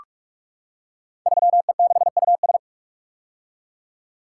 Morse code, heard as a keyed tone, about 44 wpm, 700 Hz. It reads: 3E6US
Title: Code Groups - 5 Letter:Numbers - 45wpm